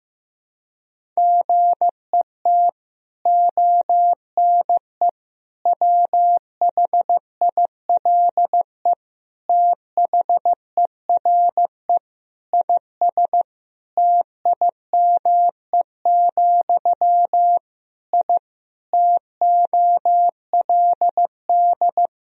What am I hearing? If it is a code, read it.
Morse code, heard as a keyed tone, about 15 wpm, 700 Hz. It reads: GET ONE WHILE THERE IS TIME, I TOLD
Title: Sentences from Top 1000 Words - Review Part 1 - 15wpm